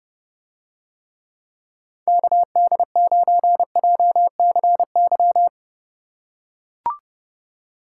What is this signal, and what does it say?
Morse code, heard as a keyed tone, about 30 wpm, 700 Hz. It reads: KD9JCY